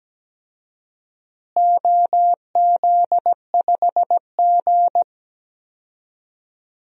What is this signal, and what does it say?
Morse code, heard as a keyed tone, about 17 wpm, 700 Hz. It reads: OZ5G
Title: DXCC-100 Call Signs - 4 Characters - Repeated 3x - 17wpm